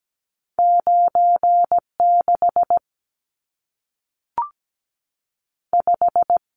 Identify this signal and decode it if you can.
Morse code, heard as a keyed tone, about 17 wpm, 700 Hz. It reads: 9 6 5
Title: Numbers - 2-Digit 17wpm